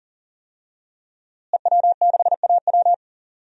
Morse code, heard as a keyed tone, about 40 wpm, 700 Hz. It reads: EW6AW
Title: DXCC-100 Call Signs - 5 Characters - Repeated 2x - 40wpm